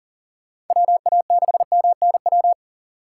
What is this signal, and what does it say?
Morse code, heard as a keyed tone, about 40 wpm, 700 Hz. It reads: WA6MNW
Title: USA Call Signs - 6 Characters - 3x Character Spacing - 40wpm